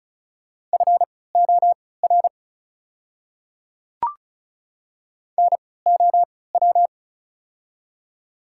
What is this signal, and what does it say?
Morse code, heard as a keyed tone, about 36 wpm, spaced as for 20 wpm, 690 Hz. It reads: FOR NOW
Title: 3x Character Spacing - Top 100 Words - 35wpm